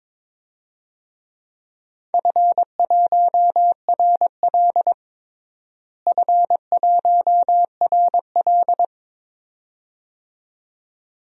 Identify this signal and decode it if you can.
Morse code, heard as a keyed tone, about 22 wpm, 700 Hz. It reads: F1RL F1RL
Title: DXCC-100 Call Signs - 4 Characters - Repeated 2x - 22wpm